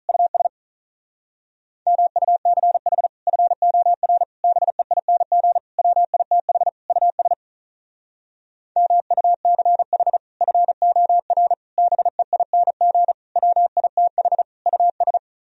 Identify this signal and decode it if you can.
Morse code, heard as a keyed tone, about 43 wpm, 700 Hz. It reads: US MUCH FOR BEING WITH US MUCH FOR BEING WITH US
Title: Speed-Racing - Sets of 5 Words 35wpm